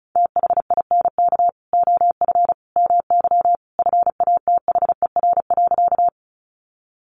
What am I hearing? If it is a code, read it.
Morse code, heard as a keyed tone, about 34 wpm, 700 Hz. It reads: THINK OF MY FATHER.
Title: Sentences from Top 300 Words 35wpm